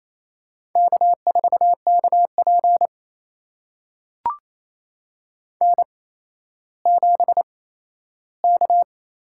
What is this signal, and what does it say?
Morse code, heard as a keyed tone, about 28 wpm, spaced as for 14 wpm, 700 Hz. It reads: K4KP N7K
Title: USA Call Signs - 4 Characters - 8x Character Spacing - 28wpm